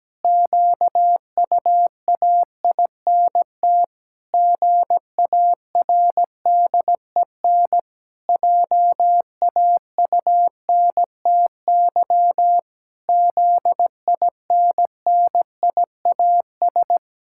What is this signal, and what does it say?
Morse code, heard as a keyed tone, about 17 wpm, 700 Hz. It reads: QUAINT GARDEN JAUNTY ZINNIAS
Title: Pangram Challenge v2 - 17wpm